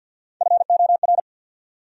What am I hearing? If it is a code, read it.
Morse code, heard as a keyed tone, about 66 wpm, 700 Hz. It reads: FOR